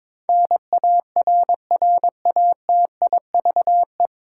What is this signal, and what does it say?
Morse code, heard as a keyed tone, about 22 wpm, 700 Hz. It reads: NARRATIVE
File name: 9 Letter Words - First 3 Letters Given - 22wpm